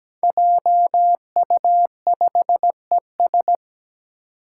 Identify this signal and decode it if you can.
Morse code, heard as a keyed tone, about 17 wpm, 700 Hz. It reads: JU5ES
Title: Five Challenging Letter-Number Combinations 17wpm